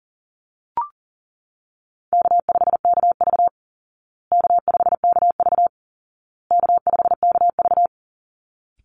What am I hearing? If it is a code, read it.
Morse code, heard as a keyed tone, about 40 wpm, 700 Hz. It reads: K5KV K5KV K5KV
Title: Call Signs Repeated 2x - 40wpm